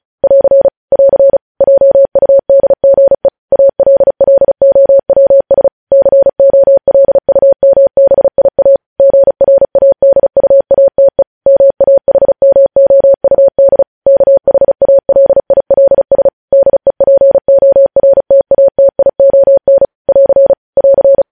Morse code, heard as a keyed tone, about 36 wpm, 550 Hz. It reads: <AR> <AR> JUDGE ALLOWS COLUMBIA GRADUATE MAHMOUD KHALILS DEPORTATION <AR> <AR>
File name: podcast / News headlines-1x-35x35